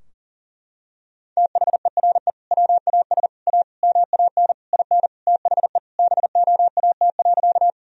Morse code, heard as a keyed tone, about 40 wpm, 700 Hz. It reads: THERE WAS A MAN IN THE BOAT.
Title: Sentences from Top 1000 Words - Review Part 3 - 40wpm